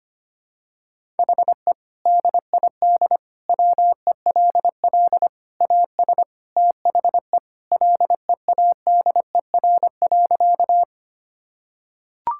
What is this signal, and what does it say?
Morse code, heard as a keyed tone, about 25 wpm, 700 Hz. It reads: HE DID WELL AS THE LEADER.
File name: practice / 25wpm / Sentences with 3-Word Phrases - 25wpm